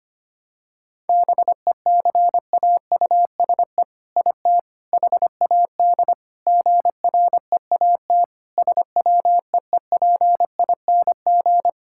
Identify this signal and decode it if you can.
Morse code, heard as a keyed tone, about 25 wpm, 700 Hz. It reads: BECAUSE IT HAD GREAT SWEEPING
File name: practice / 25wpm / All-Age Appropriate Jokes - 25wpm